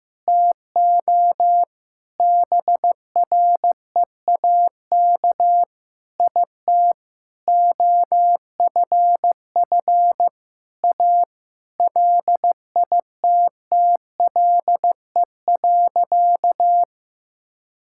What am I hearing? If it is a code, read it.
Morse code, heard as a keyed tone, about 15 wpm, 700 Hz. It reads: TO BREAK IT OFF A LITTLE.
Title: Sentences from Top 1000 Words - Review Part 6 - 15wpm